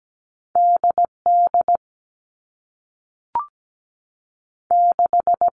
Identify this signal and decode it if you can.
Morse code, heard as a keyed tone, about 17 wpm, 700 Hz. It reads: D D 6
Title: All Cut Numbers - 2-Digit 17wpm